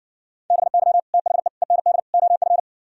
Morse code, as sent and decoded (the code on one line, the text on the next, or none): -... -.-- / - .... / . -. -.. / --- ..-.
BY THE END OF